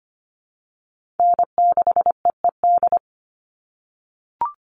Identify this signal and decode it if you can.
Morse code, heard as a keyed tone, about 25 wpm, 700 Hz. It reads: N6EED
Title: Call Signs - 25wpm